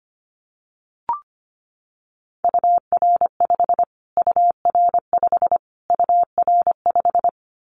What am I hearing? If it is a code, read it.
Morse code, heard as a keyed tone, about 25 wpm, 700 Hz. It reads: UR5 UR5 UR5
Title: Callsign Prefixes for Top 25 DXCC - Repeated 3x - 25wpm